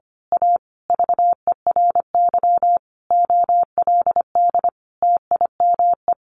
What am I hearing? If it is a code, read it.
Morse code, heard as a keyed tone, about 25 wpm, 700 Hz. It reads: A VERY OLD TIME
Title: A Sound Of Thunder 25wpm